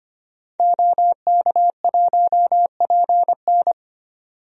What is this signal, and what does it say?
Morse code, heard as a keyed tone, about 25 wpm, 700 Hz. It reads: OK1PN
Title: DXCC-100 Call Signs - 5 Characters - 25wpm